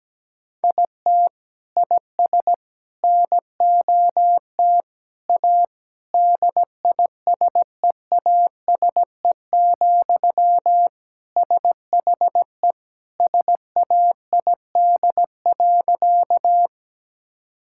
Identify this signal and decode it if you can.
Morse code, heard as a keyed tone, about 17 wpm, 700 Hz. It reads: IT IS NOT A DISEASE, SHE SAID.
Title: Sentences from Top 1000 Words - Review Part 6 - 17wpm